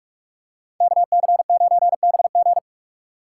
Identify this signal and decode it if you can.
Morse code, heard as a keyed tone, about 44 wpm, 700 Hz. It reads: KC9BG
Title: K1USN SST QSO Practice - 45wpm